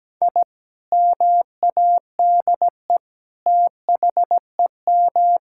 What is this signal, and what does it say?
Morse code, heard as a keyed tone, about 17 wpm, 700 Hz. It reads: I MADE THEM
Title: Sentences from Top 1000 Words - Review Part 6 - 17wpm